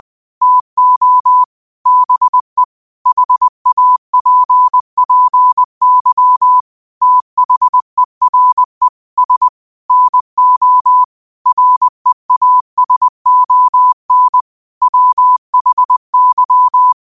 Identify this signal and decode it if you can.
Morse code, heard as a keyed tone, about 20 wpm, 1000 Hz. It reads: TO BE HAPPY THERE S NO REASON WHY
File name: qotdfiles / 20wpm